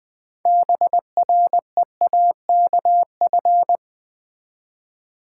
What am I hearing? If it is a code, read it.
Morse code, heard as a keyed tone, about 20 wpm, 700 Hz. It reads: BREAKF
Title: Build A Word - 3 Letters At A Time - 9-Letter Words - 20wpm